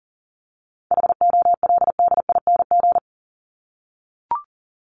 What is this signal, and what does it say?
Morse code, heard as a keyed tone, about 40 wpm, 700 Hz. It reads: HOLDING
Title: ING Suffix - Top 100 Words with Base Words 40wpm